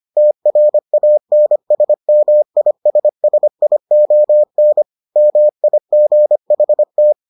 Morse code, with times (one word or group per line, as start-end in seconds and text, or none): 0.2-4.8 TRANSMISSION
5.2-7.1 MIGHT